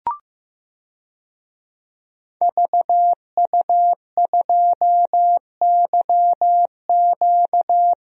Morse code, heard as a keyed tone, about 15 wpm, 700 Hz. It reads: VU2YQ
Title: DXCC-100 Call Signs - 5 Characters - Repeated 3x - 15wpm